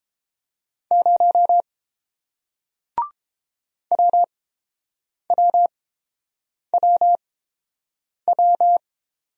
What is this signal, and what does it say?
Morse code, heard as a keyed tone, about 31 wpm, spaced as for 7 wpm, 700 Hz. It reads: 0 WWWW